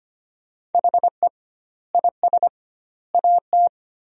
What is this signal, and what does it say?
Morse code, heard as a keyed tone, about 25 wpm, 700 Hz. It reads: HE IS AT
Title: 2x Word-Spacing - Sentences from Top 300 Words - 25wpm